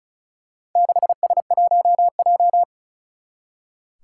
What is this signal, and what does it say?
Morse code, heard as a keyed tone, about 36 wpm, 700 Hz. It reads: 6S1J
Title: Four Challenging Letter-Number Combinations 35wpm